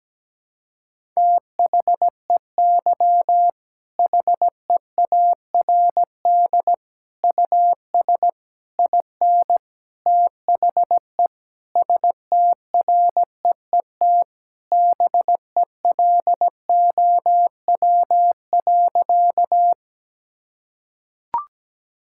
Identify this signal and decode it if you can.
Morse code, heard as a keyed tone, about 17 wpm, 700 Hz. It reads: THEY HEARD US IN THE STREET BELOW.
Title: Sentences from Top 1000 Words - Review Part 5 - 17wpm